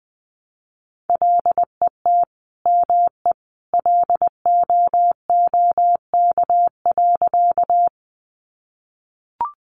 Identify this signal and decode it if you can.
Morse code, heard as a keyed tone, about 20 wpm, 700 Hz. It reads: LET ME LOOK.
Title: Sentences from Top 300 Words 20wpm